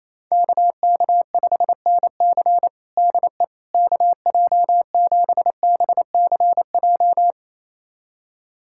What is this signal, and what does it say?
Morse code, heard as a keyed tone, about 28 wpm, 700 Hz. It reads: KK5NC DE KJ7BCJ
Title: USA Callsign DE Callsign 28wpm